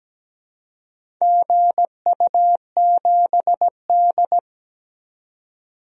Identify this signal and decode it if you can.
Morse code, heard as a keyed tone, about 17 wpm, 700 Hz. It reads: GU7D
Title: DXCC-100 Call Signs - 4 Characters - 17wpm